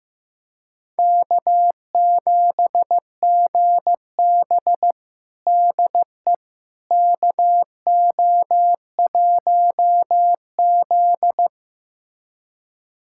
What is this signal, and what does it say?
Morse code, heard as a keyed tone, about 15 wpm, 700 Hz. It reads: K7GB DE KO1Z
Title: USA Callsign DE Callsign - 4 Characters - 15wpm